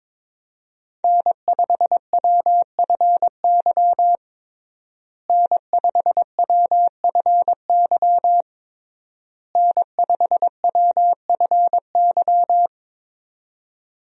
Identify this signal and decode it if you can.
Morse code, heard as a keyed tone, about 22 wpm, 700 Hz. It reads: N5WFY N5WFY N5WFY
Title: USA Call Signs - 5 Characters - Repeated 3x - 22wpm